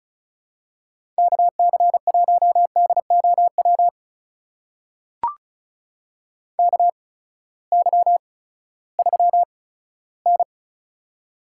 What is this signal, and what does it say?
Morse code, heard as a keyed tone, about 36 wpm, spaced as for 18 wpm, 700 Hz. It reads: KC1DOW KY3N
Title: USA Call Signs - 8x Character Spacing - 35wpm